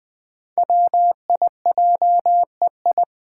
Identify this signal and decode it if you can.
Morse code, heard as a keyed tone, about 20 wpm, 700 Hz. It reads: WIJEI
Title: Five Challenging Letter-Number Combinations 20wpm